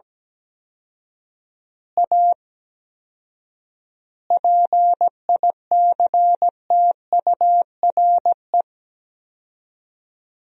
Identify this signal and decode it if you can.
Morse code, heard as a keyed tone, about 17 wpm, 700 Hz. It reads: A PICTURE
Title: 4x Word-Spacing - Sentences from Top 500 Words - 17wpm